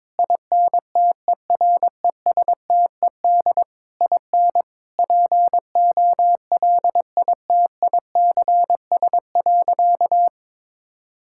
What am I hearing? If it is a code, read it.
Morse code, heard as a keyed tone, about 22 wpm, 700 Hz. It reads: INTERESTED IN POLITICS.
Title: Sentences from Top 1000 Words - Review Part 6 - 22wpm